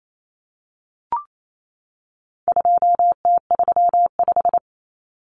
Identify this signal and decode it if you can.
Morse code, heard as a keyed tone, about 28 wpm, 700 Hz. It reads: 2T35
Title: Common Cut Numbers - 4-Digit 28wpm